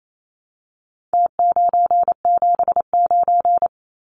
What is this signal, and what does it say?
Morse code, heard as a keyed tone, about 28 wpm, 700 Hz. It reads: T979
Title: Numbers - Contest Style - 4-Digit 28wpm